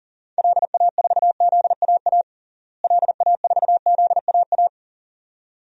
Morse code, heard as a keyed tone, about 40 wpm, 700 Hz. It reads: LA4ZAA LA4ZAA
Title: DXCC-100 Call Signs - 6 Characters - Repeated 2x - 40wpm